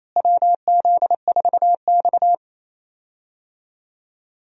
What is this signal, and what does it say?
Morse code, heard as a keyed tone, about 28 wpm, 700 Hz. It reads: WZ4X
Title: USA Call Signs - 4 Characters - Repeated 3x - 28wpm